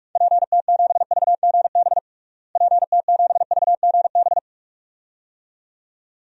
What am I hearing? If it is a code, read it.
Morse code, heard as a keyed tone, about 41 wpm, 700 Hz. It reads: PT7VGB PT7VGB
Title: DXCC-100 Call Signs - 6 Characters - Repeated 2x - 45wpm